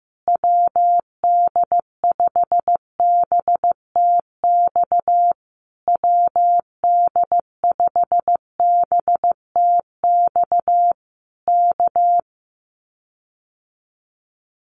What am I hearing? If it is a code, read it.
Morse code, heard as a keyed tone, about 15 wpm, 700 Hz. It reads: WD5BTX WD5BTX K
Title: Common QSO Phrases Encore 15wpm